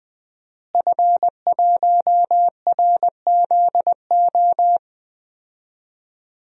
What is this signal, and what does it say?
Morse code, heard as a keyed tone, about 20 wpm, 700 Hz. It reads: F1RZO
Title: DXCC-100 Call Signs - 5 Characters - Repeated 2x - 20wpm